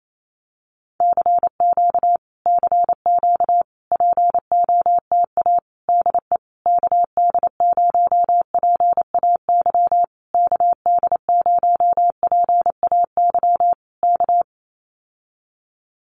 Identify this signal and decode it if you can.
Morse code, heard as a keyed tone, about 28 wpm, 700 Hz. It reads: CQ CQ POTA DE KD0PAY KD0PAY K
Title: Common QSO Phrases Encore 28wpm